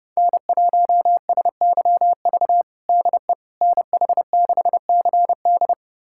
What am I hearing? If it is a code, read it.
Morse code, heard as a keyed tone, about 32 wpm, 700 Hz. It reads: N1SYV DE NH6CD
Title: USA Callsign DE Callsign - 5 Characters - 30wpm